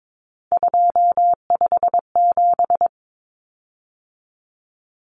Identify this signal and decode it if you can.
Morse code, heard as a keyed tone, about 22 wpm, 700 Hz. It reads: 257
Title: Common Cut Numbers - 3-Digit 22wpm